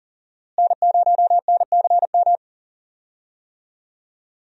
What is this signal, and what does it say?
Morse code, heard as a keyed tone, about 37 wpm, 700 Hz. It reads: N0NCM
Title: USA Call Signs - 5 Characters - Repeated 3x - 40wpm